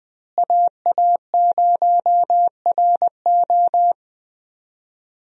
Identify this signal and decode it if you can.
Morse code, heard as a keyed tone, about 20 wpm, 700 Hz. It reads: AA0RO
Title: USA Call Signs - 5 Characters - Repeated 3x - 20wpm